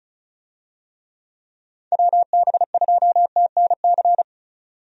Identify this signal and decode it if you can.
Morse code, heard as a keyed tone, about 36 wpm, 700 Hz. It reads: WB2TNC
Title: USA Call Signs - 6 Characters - 35wpm